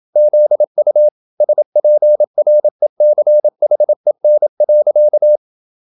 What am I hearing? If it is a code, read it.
Morse code, heard as a keyed tone, about 27 wpm, 600 Hz. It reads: ZU SPRECHEN.